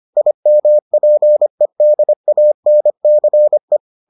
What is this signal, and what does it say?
Morse code, heard as a keyed tone, about 25 wpm, 600 Hz. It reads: IMPEDANCE